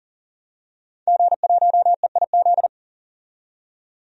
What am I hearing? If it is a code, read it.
Morse code, heard as a keyed tone, about 40 wpm, 700 Hz. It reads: G1EIZ
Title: DXCC-100 Call Signs - 5 Characters - 40wpm